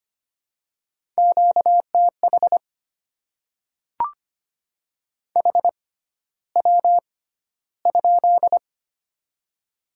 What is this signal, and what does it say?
Morse code, heard as a keyed tone, about 25 wpm, spaced as for 13 wpm, 700 Hz. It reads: QTH HW?
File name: Top 20 QSO Elements - 6x Character Spacing - 25wpm